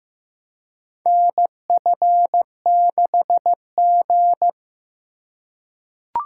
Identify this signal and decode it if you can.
Morse code, heard as a keyed tone, about 15 wpm, 700 Hz. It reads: NF6G
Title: USA Call Signs - 4 Characters - 3x Character Spacing - 15wpm